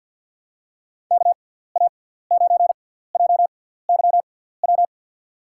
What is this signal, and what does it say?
Morse code, heard as a keyed tone, about 52 wpm, spaced as for 17 wpm, 700 Hz. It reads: KA9JYW
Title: USA Call Signs - 6 Characters - 6x Character Spacing - 50wpm